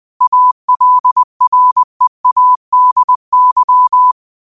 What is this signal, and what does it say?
Morse code, heard as a keyed tone, about 20 wpm, 1000 Hz. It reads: ALREADY
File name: qotdfiles / 20wpm